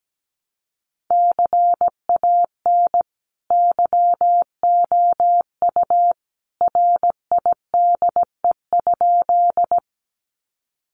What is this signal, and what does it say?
Morse code, heard as a keyed tone, about 17 wpm, 700 Hz. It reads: CAN YOU RIDE?
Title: Sentences from Top 1200 Words 17wpm